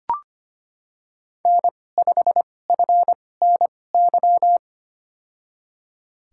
Morse code, heard as a keyed tone, about 25 wpm, spaced as for 18 wpm, 700 Hz. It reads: N5FNY
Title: USA Call Signs - 5 Characters - 2x Character Spacing - 25wpm